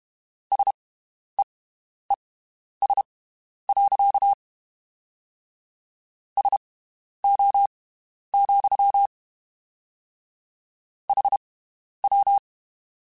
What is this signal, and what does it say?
Morse code, heard as a keyed tone, about 32 wpm, spaced as for 11 wpm, 800 Hz. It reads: SEES. SO, HW
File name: (spaced) / QSOs Spaced-mp3s / WD0FBT 1 @30 sp